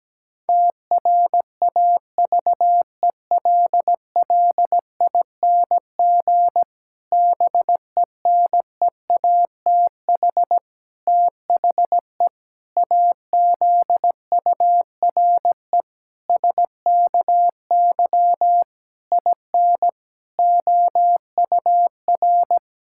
Morse code, heard as a keyed tone, about 17 wpm, 700 Hz. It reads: TRAVELLING BENEATH THE AZURE SKY IN OUR
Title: Pangram Challenge v2 - 17wpm